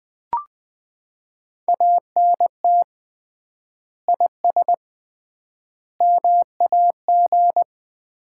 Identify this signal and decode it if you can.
Morse code, heard as a keyed tone, about 19 wpm, 700 Hz. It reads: ANT IS MAG
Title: Common QSO Phrases - 3x Word-Spacing - 20wpm